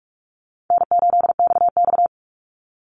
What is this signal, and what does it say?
Morse code, heard as a keyed tone, about 44 wpm, 700 Hz. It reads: N8XX
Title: Easy Call Signs Repeated 2x - 45wpm